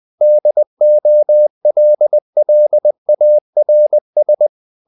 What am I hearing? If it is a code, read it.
Morse code, heard as a keyed tone, about 20 wpm, 600 Hz. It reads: DOLLARS
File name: the old man and the sea / the old man and the sea part 1 20wpm